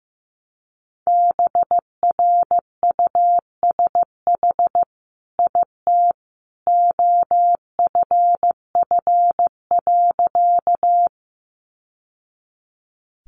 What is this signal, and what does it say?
Morse code, heard as a keyed tone, about 15 wpm, 700 Hz. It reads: BRUSH IT OFF.
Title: Sentences from Top 1100 Words 15wpm